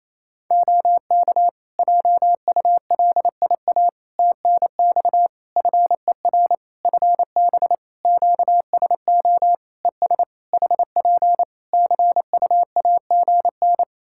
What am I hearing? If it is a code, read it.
Morse code, heard as a keyed tone, about 28 wpm, 700 Hz. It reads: OK JULIA TNX FER FB QSO ES HP CUAGN
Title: Practice QSO - Exchange 4 - Ending - 28wpm